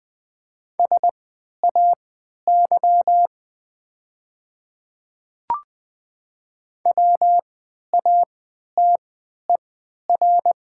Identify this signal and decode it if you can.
Morse code, heard as a keyed tone, about 20 wpm, spaced as for 11 wpm, 700 Hz. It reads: SAY WATER
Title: 3x Character Spacing - Top 200 Words - 20wpm